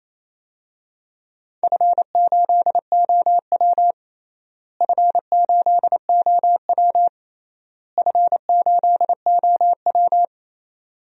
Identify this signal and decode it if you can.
Morse code, heard as a keyed tone, about 28 wpm, 700 Hz. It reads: F8OW F8OW F8OW
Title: DXCC-100 Call Signs - 4 Characters - Repeated 3x - 28wpm